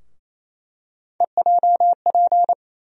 Morse code, heard as a keyed tone, about 28 wpm, 700 Hz. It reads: EJP
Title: USA Call Sign Postfixes - 3 Letters 28wpm